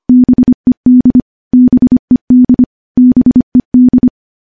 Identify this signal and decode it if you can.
Morse code, heard as a keyed tone, about 25 wpm, 250 Hz. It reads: BED BED BED
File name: morse / morse-training-25WPM